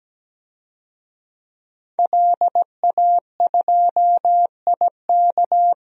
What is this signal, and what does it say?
Morse code, heard as a keyed tone, about 17 wpm, 700 Hz. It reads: LA2IK
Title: DXCC-100 Call Signs - 5 Characters - Repeated 2x - 17wpm